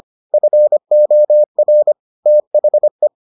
Morse code, heard as a keyed tone, about 25 wpm, 600 Hz. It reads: FOR THE